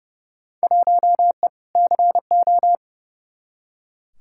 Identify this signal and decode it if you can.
Morse code, heard as a keyed tone, about 30 wpm, 700 Hz. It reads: 1E CO
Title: Field Day Exchanges - 30wpm